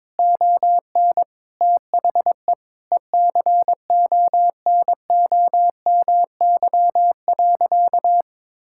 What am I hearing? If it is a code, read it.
Morse code, heard as a keyed tone, about 22 wpm, 700 Hz. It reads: ON THE ECONOMY.
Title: Sentences from Top 1000 Words - Review Part 6 - 22wpm